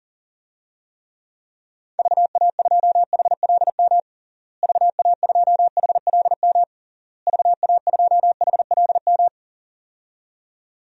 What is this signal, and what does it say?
Morse code, heard as a keyed tone, about 40 wpm, 700 Hz. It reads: VA2HLM VA2HLM VA2HLM
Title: DXCC-100 Call Signs - 6 Characters - Repeated 3x - 40wpm